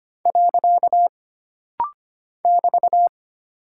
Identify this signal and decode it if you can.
Morse code, heard as a keyed tone, about 25 wpm, 700 Hz. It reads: . <BT>
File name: Single Punctuation-Prosign - Rapid-Fire - 25wpm